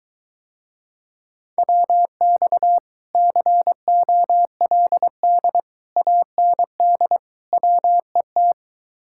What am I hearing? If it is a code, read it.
Morse code, heard as a keyed tone, about 24 wpm, 700 Hz. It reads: WX COLD AND WET